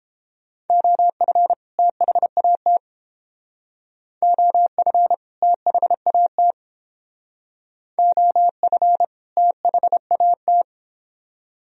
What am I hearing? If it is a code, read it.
Morse code, heard as a keyed tone, about 32 wpm, 700 Hz. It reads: OF THAT OF THAT OF THAT